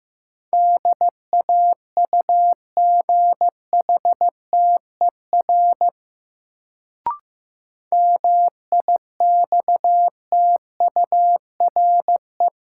Morse code, heard as a keyed tone, about 15 wpm, 700 Hz. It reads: DAUGHTER MIXTURE
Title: Sentences from Top 1500 Words - New Word Drill Alt - 15wpm